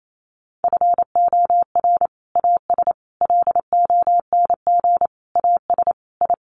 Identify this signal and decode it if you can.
Morse code, heard as a keyed tone, about 28 wpm, 700 Hz. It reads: FOR AS LONG AS I
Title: Sets of 5 Words Encore2 28wpm